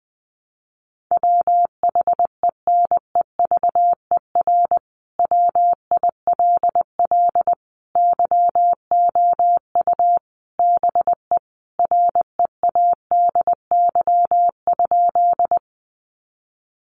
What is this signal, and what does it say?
Morse code, heard as a keyed tone, about 20 wpm, 700 Hz. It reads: WHENEVER WILL YOU BE READY?
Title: Sentences from Top 1200 Words 20wpm